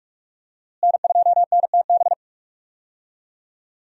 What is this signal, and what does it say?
Morse code, heard as a keyed tone, about 46 wpm, 700 Hz. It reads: N2NTB
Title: USA Call Signs - 5 Characters - Repeated 3x - 45wpm